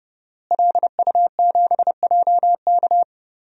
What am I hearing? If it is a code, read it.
Morse code, heard as a keyed tone, about 30 wpm, 700 Hz. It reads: LU7JK
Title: DXCC-100 Call Signs - 5 Characters - Repeated 2x - 30wpm